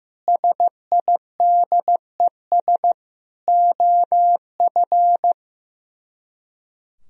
Morse code, heard as a keyed tone, about 16 wpm, 700 Hz. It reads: SIDES OF
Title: Speed-Racing - Sets of 4 Words 15wpm